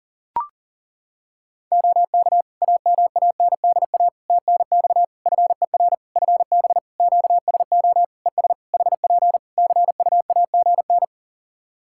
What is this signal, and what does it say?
Morse code, heard as a keyed tone, about 40 wpm, 700 Hz. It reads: OK AMANDA TNX FER FB QSO ES HP CUAGN
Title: K1USN SST QSO Practice 40wpm